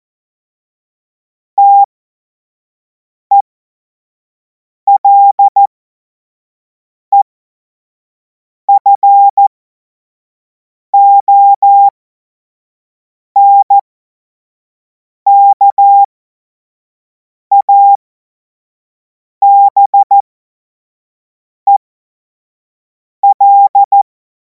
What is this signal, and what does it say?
Morse code, heard as a keyed tone, about 14 wpm, spaced as for 5 wpm, 800 Hz. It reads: TELEFONKABEL
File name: Morsekurs / Witze / Witz 10. März 2026 25 F70